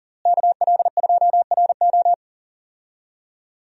Morse code, heard as a keyed tone, about 37 wpm, 700 Hz. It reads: KL2RO
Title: USA Call Signs - 5 Characters - Repeated 3x - 40wpm